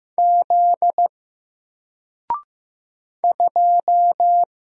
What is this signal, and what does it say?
Morse code, heard as a keyed tone, about 15 wpm, 700 Hz. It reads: Z 2
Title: Single Letter-Number Alt 15wpm